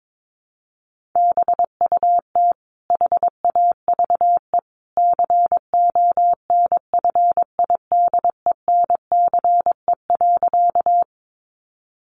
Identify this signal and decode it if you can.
Morse code, heard as a keyed tone, about 22 wpm, 700 Hz. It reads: BUT HAVE CONFIDENCE.
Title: Sentences from Top 1500 Words 22wpm